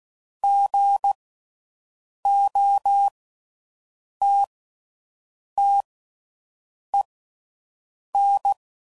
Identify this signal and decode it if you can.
Morse code, heard as a keyed tone, about 16 wpm, spaced as for 6 wpm, 800 Hz. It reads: GOTTEN